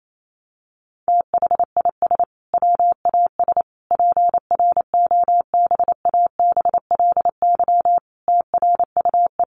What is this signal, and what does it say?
Morse code, heard as a keyed tone, about 28 wpm, 700 Hz. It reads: THIS WAS PROBABLY TRUE
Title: Sentences from Top 700 Words 28wpm